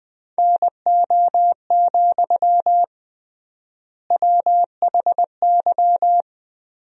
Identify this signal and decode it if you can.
Morse code, heard as a keyed tone, about 21 wpm, 700 Hz. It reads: NO, WHY
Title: 3x Word-Spacing - Sentences from Top 300 Words - 20wpm